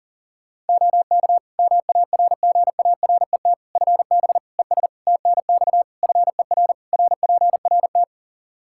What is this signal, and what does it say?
Morse code, heard as a keyed tone, about 40 wpm, 700 Hz. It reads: OK MARGARET FB ES TNX FER RPRT
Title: Practice QSO - Exchange 2 - Info - 40wpm